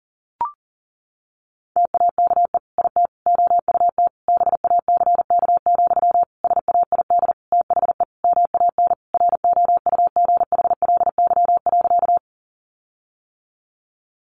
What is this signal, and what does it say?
Morse code, heard as a keyed tone, about 40 wpm, 700 Hz. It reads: TAKE IT OUT BACK, SAID THE MAN ROUGHLY.
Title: Sentences from Top 1500 Words 40wpm